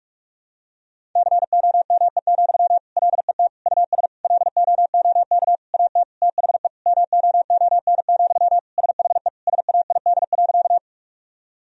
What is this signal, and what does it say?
Morse code, heard as a keyed tone, about 46 wpm, 700 Hz. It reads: COME, LET US LOOK AT THE MOON, SHE SAID.
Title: Sentences from Top 1000 Words - Review Part 3 - 45wpm